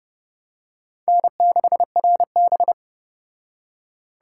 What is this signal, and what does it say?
Morse code, heard as a keyed tone, about 30 wpm, 700 Hz. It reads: N6RB